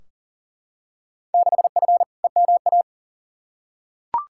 Morse code, heard as a keyed tone, about 40 wpm, 700 Hz. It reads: 6F EMA
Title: Field Day Exchanges Encore - 40wpm